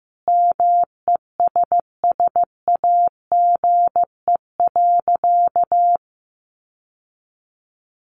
Morse code, heard as a keyed tone, about 15 wpm, 700 Hz. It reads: MESSAGE.
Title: Sentences from Top 1100 Words 15wpm